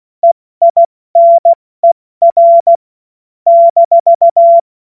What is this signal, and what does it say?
Morse code, heard as a keyed tone, about 16 wpm, spaced as for 14 wpm, 680 Hz. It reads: EINER -
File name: Morsekurs / Witze / Witz 01. November 2022 70 F80